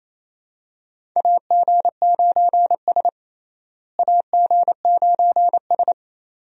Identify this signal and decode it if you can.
Morse code, heard as a keyed tone, about 28 wpm, 700 Hz. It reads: AG9S AG9S
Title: USA Call Signs - 4 Characters - Repeated 2x - 28wpm